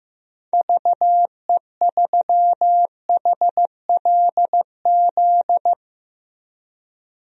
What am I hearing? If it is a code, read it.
Morse code, heard as a keyed tone, about 15 wpm, 700 Hz. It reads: VE3HLZ
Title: DXCC-100 Call Signs - 6 Characters - 15wpm